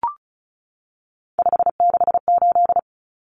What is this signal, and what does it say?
Morse code, heard as a keyed tone, about 34 wpm, 700 Hz. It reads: 568
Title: Numbers - 3-Digit 35wpm